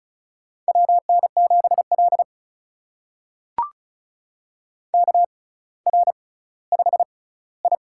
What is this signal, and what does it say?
Morse code, heard as a keyed tone, about 36 wpm, spaced as for 18 wpm, 700 Hz. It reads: WN7L KR5I
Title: USA Call Signs - 4 Characters - 6x Character Spacing - 35wpm